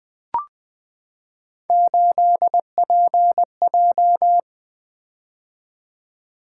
Morse code, heard as a keyed tone, about 20 wpm, 700 Hz. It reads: 8PJ
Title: Three Letter-Number Combinations - 20wpm